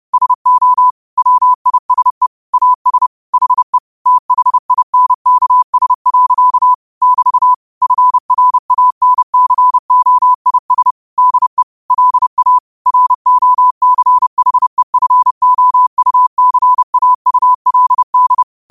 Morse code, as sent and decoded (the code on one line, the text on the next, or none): ... --- / .-- .. ... / .- ... / .... / - .... .. -. -.- ... .-.-.- / -...- / ..-. .-. .- -. -.-. --- .. ... / -.. / .-.. .- / .-. --- -.-. .... ..-. --- ..- -.-. .- ..- .-.. -..
SO WISE AS HE THINKS. <BT> FRANCOIS DE LA ROCHEFOUCAULD